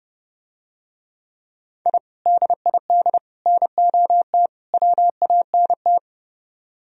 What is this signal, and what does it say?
Morse code, heard as a keyed tone, about 31 wpm, 700 Hz. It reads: I DID NOT WANT